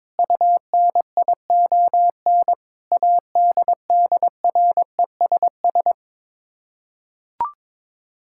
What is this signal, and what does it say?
Morse code, heard as a keyed tone, about 22 wpm, 700 Hz. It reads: UNION ADDRESS
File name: practice / 22wpm / Sets of 7 Words - 22wpm